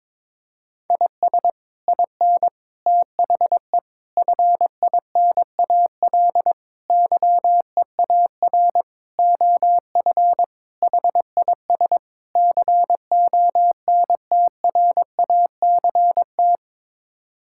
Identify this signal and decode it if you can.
Morse code, heard as a keyed tone, about 22 wpm, 700 Hz. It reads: IS IN THE FINAL YEAR OF HIS CONTRACT
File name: practice / 22wpm / Sets of 8 Words - 22wpm